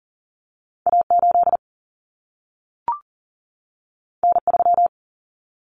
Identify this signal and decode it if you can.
Morse code, heard as a keyed tone, about 40 wpm, 700 Hz. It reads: A8 N3
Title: Common Cut Numbers - 2-Digit 40wpm